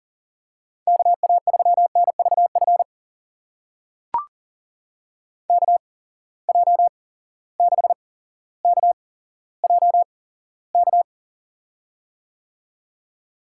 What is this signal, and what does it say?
Morse code, heard as a keyed tone, about 40 wpm, spaced as for 17 wpm, 700 Hz. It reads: KA3NVF KJ6KJK
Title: USA Call Signs - 6 Characters - 8x Character Spacing - 40wpm